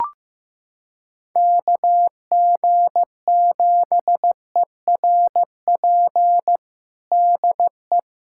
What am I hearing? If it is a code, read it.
Morse code, heard as a keyed tone, about 15 wpm, 700 Hz. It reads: KG7ERP DE
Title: USA Callsign DE Callsign 15wpm